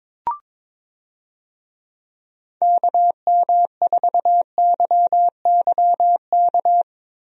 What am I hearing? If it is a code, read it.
Morse code, heard as a keyed tone, about 22 wpm, 700 Hz. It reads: KM4YYK
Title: USA Call Signs - 6 Characters - Repeated 2x - 22wpm